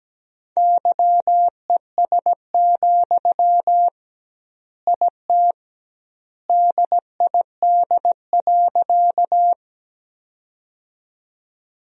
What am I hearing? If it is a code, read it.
Morse code, heard as a keyed tone, about 17 wpm, 700 Hz. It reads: YES, IT DID.
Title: 2x Word-Spacing - Sentences from Top 500 Words - 17wpm